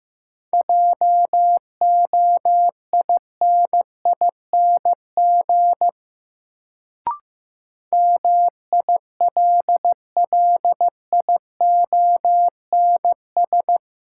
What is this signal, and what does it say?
Morse code, heard as a keyed tone, about 15 wpm, 700 Hz. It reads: JOINING MILLIONS
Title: Sets of 5 Words - New Word Drill - 15wpm